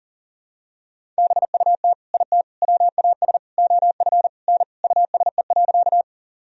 Morse code, heard as a keyed tone, about 40 wpm, 700 Hz. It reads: BUT IT WAS OF N USE.
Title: Sentences from Top 100 Words and Abbreviations 40wpm